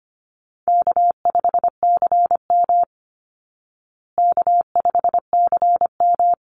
Morse code, heard as a keyed tone, about 25 wpm, 700 Hz. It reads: K5CM K5CM
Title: Call Signs Repeated 3x - 25wpm